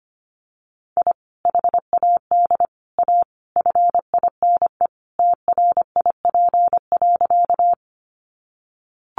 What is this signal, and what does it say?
Morse code, heard as a keyed tone, about 25 wpm, 700 Hz. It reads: I HAD A FINE TRIP.
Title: Sentences from Top 1300 Words 25wpm